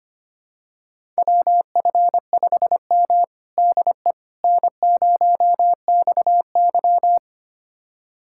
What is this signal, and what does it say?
Morse code, heard as a keyed tone, about 25 wpm, 700 Hz. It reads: WF5M DE N0XY
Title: USA Callsign DE Callsign - 4 Characters - 25wpm